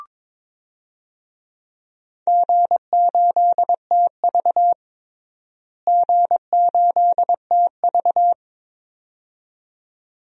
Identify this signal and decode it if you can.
Morse code, heard as a keyed tone, about 22 wpm, 700 Hz. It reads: G8TV G8TV
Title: DXCC-100 Call Signs - 4 Characters - Repeated 2x - 22wpm